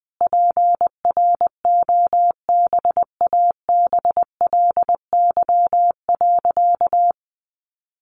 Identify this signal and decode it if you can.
Morse code, heard as a keyed tone, about 20 wpm, 700 Hz. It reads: PROBABLY.
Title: Sentences from Top 700 Words 20wpm